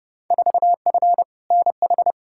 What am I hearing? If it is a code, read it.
Morse code, heard as a keyed tone, about 31 wpm, 700 Hz. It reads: 4F NH
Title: Field Day Exchanges Encore - 30wpm